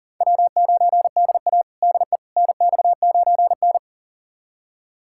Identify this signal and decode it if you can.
Morse code, heard as a keyed tone, about 40 wpm, 700 Hz. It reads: W9DA DE NX9N
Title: USA Callsign DE Callsign - 4 Characters - 40wpm